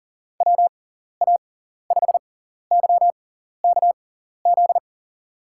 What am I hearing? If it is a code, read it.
Morse code, heard as a keyed tone, about 38 wpm, spaced as for 14 wpm, 700 Hz. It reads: WA5YKZ
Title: USA Call Signs - 6 Characters - 6x Character Spacing - 40wpm